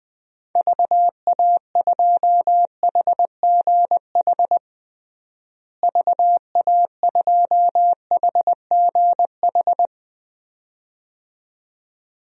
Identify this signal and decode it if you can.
Morse code, heard as a keyed tone, about 20 wpm, 700 Hz. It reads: VA2HGH VA2HGH
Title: DXCC-100 Call Signs - 6 Characters - Repeated 2x - 20wpm